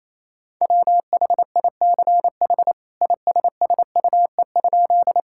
Morse code, heard as a keyed tone, about 27 wpm, 700 Hz. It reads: WHICH ISSUE?
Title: Sentences from Top 1000 Words - Review Part 6 - 28wpm